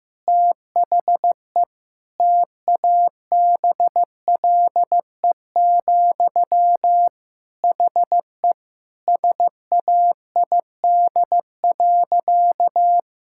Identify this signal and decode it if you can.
Morse code, heard as a keyed tone, about 15 wpm, 700 Hz. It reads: THE TABLE, HE SAID.
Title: Sentences from Top 1000 Words - Review Part 6 - 15wpm